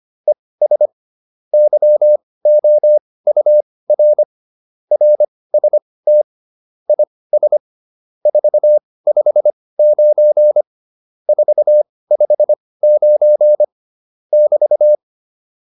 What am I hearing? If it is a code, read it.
Morse code, heard as a keyed tone, about 25 wpm, spaced as for 18 wpm, 600 Hz. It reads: ES YOUR RST IS 459 459 <BT>
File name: QSO307 18